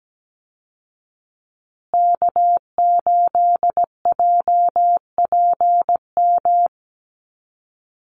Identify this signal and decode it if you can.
Morse code, heard as a keyed tone, about 17 wpm, 700 Hz. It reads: K8JPM
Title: Call Signs Repeated 2x - 17wpm